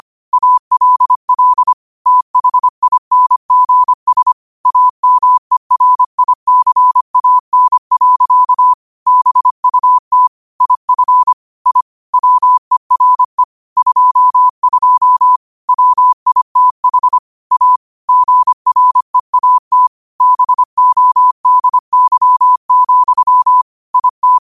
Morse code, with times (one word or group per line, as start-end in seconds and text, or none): 0.3-1.7 ALL
2.1-4.3 THINGS
4.7-8.7 AMERICAN.
9.1-10.3 BUT
10.6-11.3 IF
11.7-11.8 I
12.1-13.4 WERE
13.8-15.4 22
15.7-17.2 WITH
17.5-17.8 A
18.1-19.9 GREAT
20.2-23.6 BODY,
23.9-24.4 IT